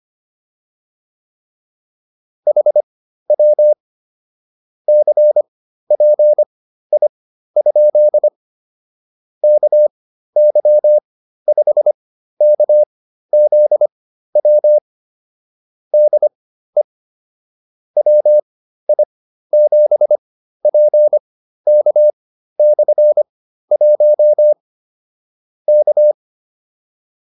Morse code, heard as a keyed tone, about 25 wpm, spaced as for 13 wpm, 600 Hz. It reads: HW CPI? KY5KZW DE WI7PK/1 K
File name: QSO304 13